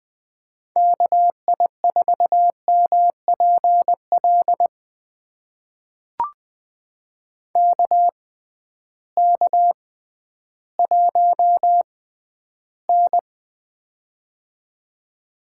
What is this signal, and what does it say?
Morse code, heard as a keyed tone, about 20 wpm, spaced as for 12 wpm, 700 Hz. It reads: KI4MPL KK1N
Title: USA Call Signs - 6x Character Spacing - 20wpm